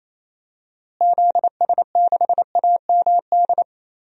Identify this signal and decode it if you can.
Morse code, heard as a keyed tone, about 28 wpm, 700 Hz. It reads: ZS6AMD
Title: DXCC-100 Call Signs - 6 Characters - Repeated 2x - 28wpm